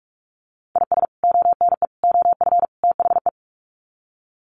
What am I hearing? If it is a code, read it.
Morse code, heard as a keyed tone, about 44 wpm, 700 Hz. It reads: IS ONE OF THE
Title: Sets of 4 Words 45wpm